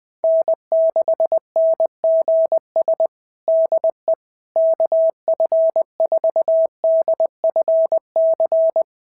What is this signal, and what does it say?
Morse code, heard as a keyed tone, about 20 wpm, 650 Hz. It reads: N6NGS DE KF4DFC
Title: SOTA QSO Practice 20wpm